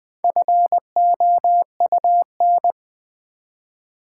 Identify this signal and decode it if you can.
Morse code, heard as a keyed tone, about 20 wpm, 700 Hz. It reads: FOUN
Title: Build A Word - 1 Letter At A Time - 3 to 5-Letter Words - 20wpm